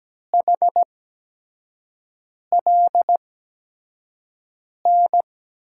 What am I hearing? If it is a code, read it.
Morse code, heard as a keyed tone, about 17 wpm, spaced as for 5 wpm, 700 Hz. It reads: HLN
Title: USA Call Signs - 5 Characters - 8x Character Spacing - 17wpm